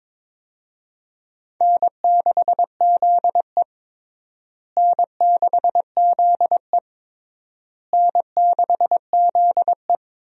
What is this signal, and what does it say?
Morse code, heard as a keyed tone, about 22 wpm, 700 Hz. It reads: N6ZE N6ZE N6ZE
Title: USA Call Signs - 4 Characters - Repeated 3x - 22wpm